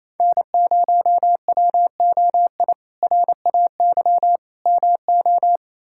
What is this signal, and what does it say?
Morse code, heard as a keyed tone, about 28 wpm, 700 Hz. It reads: N0WOI RAY MO
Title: K1USN SST QSO Practice - 28wpm